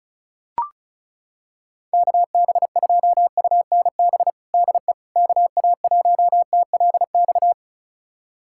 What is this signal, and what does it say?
Morse code, heard as a keyed tone, about 35 wpm, 700 Hz. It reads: KB2UNB DE KA1TLX
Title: USA Callsign DE Callsign - 6 Characters - 35wpm